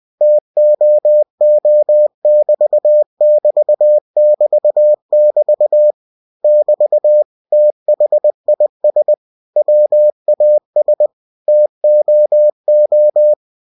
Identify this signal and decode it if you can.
Morse code, heard as a keyed tone, about 20 wpm, 600 Hz. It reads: TOO<BT><BT><BT><BT> <BT> THIS WAS TOO
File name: cw-books / winnie-the-pooh / winnie-20wpm-page-0048